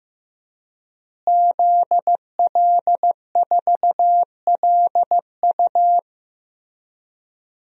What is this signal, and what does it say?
Morse code, heard as a keyed tone, about 15 wpm, 700 Hz. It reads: ZL4LU
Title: DXCC-100 Call Signs - 5 Characters - Repeated 3x - 15wpm